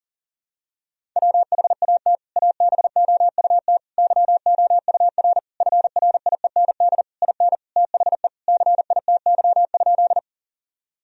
Morse code, heard as a keyed tone, about 37 wpm, 700 Hz. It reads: WHAT ABOUT YOUR FRIEND IN THE CITY?
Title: Sentences from Top 1000 Words - Review Part 1 - 40wpm